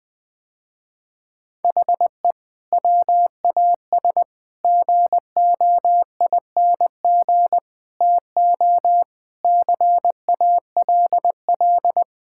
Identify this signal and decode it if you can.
Morse code, heard as a keyed tone, about 20 wpm, 700 Hz. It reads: HE WAS GOING TO CALL
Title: Sentences with 4-Word Phrases - Spoken Beginning - 20wpm